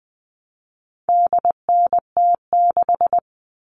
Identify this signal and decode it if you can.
Morse code, heard as a keyed tone, about 20 wpm, 700 Hz. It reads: DNT6
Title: All Cut Numbers - 4-Digit 20wpm